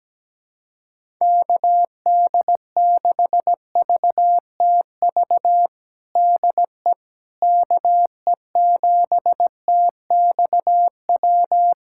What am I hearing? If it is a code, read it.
Morse code, heard as a keyed tone, about 17 wpm, 700 Hz. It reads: KD6VTV DE KE7TXW
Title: USA Callsign DE Callsign 17wpm